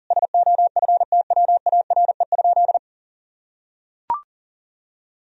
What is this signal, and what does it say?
Morse code, heard as a keyed tone, about 40 wpm, 700 Hz. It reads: SOFTWARE?
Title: Sentences from Top 1000 Words - Review Part 3 - 40wpm